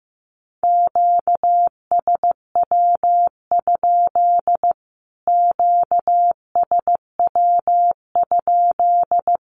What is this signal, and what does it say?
Morse code, heard as a keyed tone, about 15 wpm, 700 Hz. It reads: QSW? QSW?
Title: Q-Codes Repeated 3x 15wpm